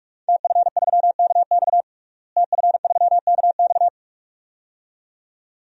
Morse code, heard as a keyed tone, about 44 wpm, 700 Hz. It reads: TF3KX TF3KX
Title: DXCC-100 Call Signs - 5 Characters - Repeated 2x - 45wpm